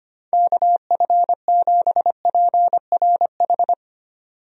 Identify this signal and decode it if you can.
Morse code, heard as a keyed tone, about 25 wpm, 700 Hz. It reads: KF7PRH
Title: USA Call Signs - 6 Characters - 25wpm